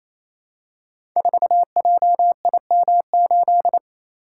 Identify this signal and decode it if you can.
Morse code, heard as a keyed tone, about 27 wpm, 700 Hz. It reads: 4JIM8
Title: Code Groups - 5 Letter:Numbers - 28wpm